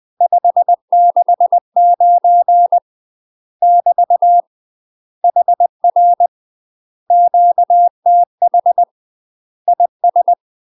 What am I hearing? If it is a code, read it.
Morse code, heard as a keyed tone, about 20 wpm, 700 Hz. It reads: 569 <BT> HR QTH IS
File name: UK MASTER QSOs TRAINING / 20WPM / UK MASTER QSOs 0011